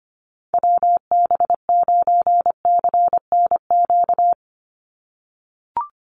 Morse code, heard as a keyed tone, about 25 wpm, 700 Hz. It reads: WB9CNQ
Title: USA Call Signs 25wpm